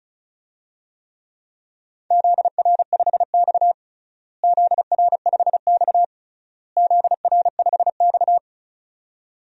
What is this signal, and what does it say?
Morse code, heard as a keyed tone, about 34 wpm, 700 Hz. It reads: ZR5X ZR5X ZR5X
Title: DXCC-100 Call Signs - 4 Characters - Repeated 3x - 35wpm